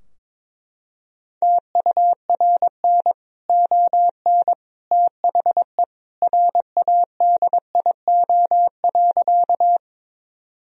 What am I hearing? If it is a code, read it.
Morse code, heard as a keyed tone, about 21 wpm, 700 Hz. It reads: TURN ON THE RADIO.
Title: Sentences from Top 1000 Words - Review Part 3 - 22wpm